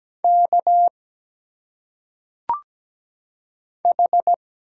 Morse code, heard as a keyed tone, about 17 wpm, 700 Hz. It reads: K H
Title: Single Letters - Phonetic Pronunciation 17wpm